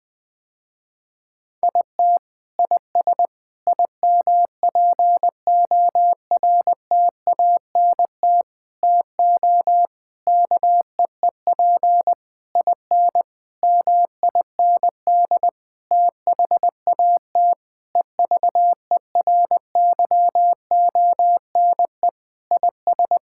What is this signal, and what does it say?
Morse code, heard as a keyed tone, about 20 wpm, 700 Hz. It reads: IT IS IMPORTANT TO KEEP IN MIND THAT EVERYONE IS
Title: Sentences with 8-Word Phrases - Spoken Beginning - 20wpm